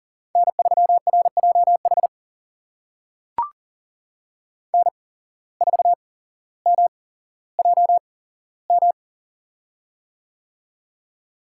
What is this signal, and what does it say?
Morse code, heard as a keyed tone, about 40 wpm, spaced as for 17 wpm, 700 Hz. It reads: N3RJH N4MJM
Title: USA Call Signs - 5 Characters - 8x Character Spacing - 40wpm